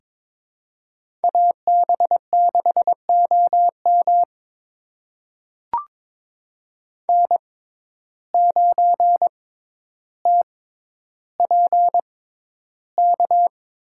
Morse code, heard as a keyed tone, about 22 wpm, spaced as for 11 wpm, 700 Hz. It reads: AB6OM N9TPK
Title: USA Call Signs - 5 Characters - 6x Character Spacing - 22wpm